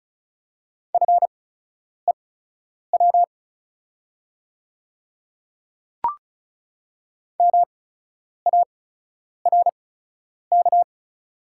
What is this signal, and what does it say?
Morse code, heard as a keyed tone, about 34 wpm, spaced as for 10 wpm, 690 Hz. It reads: FEW MARK
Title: 8x Character Spacing - Top 500 Words - 35wpm